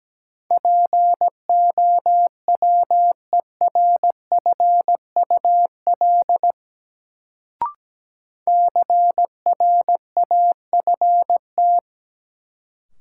Morse code, heard as a keyed tone, about 17 wpm, 700 Hz. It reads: POWERFUL CRAFT
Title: Sentences from Top 1000 Words - Review Part 6 - New Word Drill - 17wpm